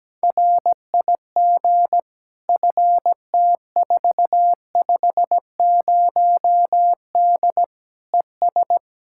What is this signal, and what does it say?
Morse code, heard as a keyed tone, about 17 wpm, 700 Hz. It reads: RIG FT450D ES
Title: K1USN SST QSO Practice 17wpm